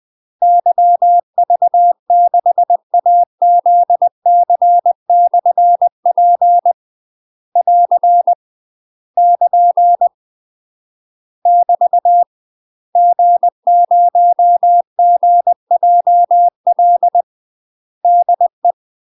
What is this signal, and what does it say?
Morse code, heard as a keyed tone, about 20 wpm, 700 Hz. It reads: YV6AZC/P <AR> <KN> <BT> G0GJL DE
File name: UK MASTER QSOs TRAINING / 20WPM / UK MASTER QSOs 0032